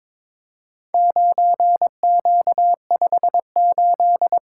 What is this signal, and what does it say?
Morse code, heard as a keyed tone, about 21 wpm, 700 Hz. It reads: 9Q58
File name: practice / 22wpm / Four Letter-Number Combinations - 22wpm